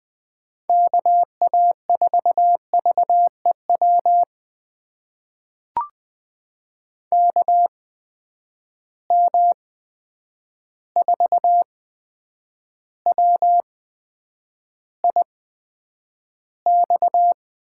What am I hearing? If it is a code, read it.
Morse code, heard as a keyed tone, about 20 wpm, spaced as for 9 wpm, 700 Hz. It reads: KA4VEW KM4WIX
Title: USA Call Signs - 6 Characters - 8x Character Spacing - 20wpm